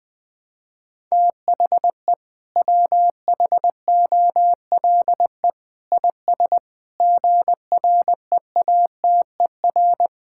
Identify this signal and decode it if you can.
Morse code, heard as a keyed tone, about 21 wpm, 700 Hz. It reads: THE WHOLE IS GREATER
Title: Sentences with 6-Word Phrases - 20wpm